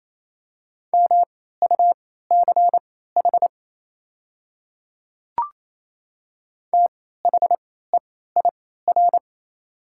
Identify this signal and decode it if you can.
Morse code, heard as a keyed tone, about 28 wpm, spaced as for 16 wpm, 700 Hz. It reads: MUCH THEIR
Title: 3x Character Spacing - Top 200 Words - 28wpm